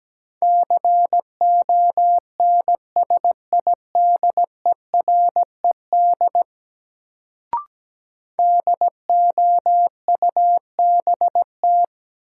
Morse code, heard as a keyed tone, about 17 wpm, 700 Hz. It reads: CONSIDERED DOUBT
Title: Sets of 5 Words - New Word Drill - 17wpm